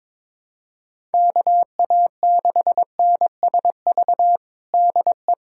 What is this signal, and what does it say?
Morse code, heard as a keyed tone, about 22 wpm, 700 Hz. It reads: KA6NSV DE
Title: USA Callsign DE Callsign 22wpm